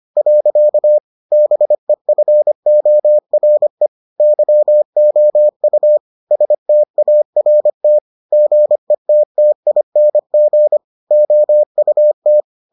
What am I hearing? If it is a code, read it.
Morse code, heard as a keyed tone, about 25 wpm, 600 Hz. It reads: . BEFORE YOU START GETTING OUT